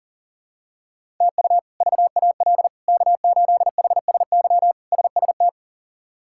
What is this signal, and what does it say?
Morse code, heard as a keyed tone, about 40 wpm, 700 Hz. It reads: TU VAL K8HSY SST
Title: K1USN SST QSO Practice - 40wpm